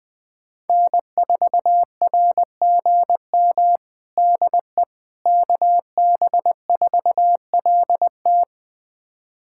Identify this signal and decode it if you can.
Morse code, heard as a keyed tone, about 20 wpm, 700 Hz. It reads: N4RGM DE KB4LT
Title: USA Callsign DE Callsign - 5 Characters - 20wpm